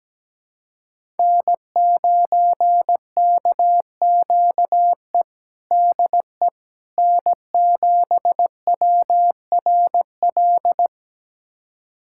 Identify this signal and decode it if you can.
Morse code, heard as a keyed tone, about 17 wpm, 700 Hz. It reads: N9KQE DE N7WRL
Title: USA Callsign DE Callsign - 5 Characters - 17wpm